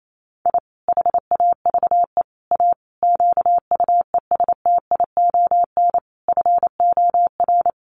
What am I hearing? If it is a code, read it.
Morse code, heard as a keyed tone, about 28 wpm, 700 Hz. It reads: I HAVE A QUESTION FOR
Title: Sets of 5 Words Encore2 28wpm